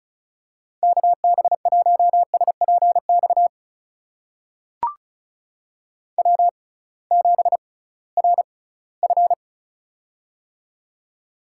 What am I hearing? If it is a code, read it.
Morse code, heard as a keyed tone, about 34 wpm, spaced as for 20 wpm, 700 Hz. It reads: KB1SPX W7RF
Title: USA Call Signs - 6x Character Spacing - 35wpm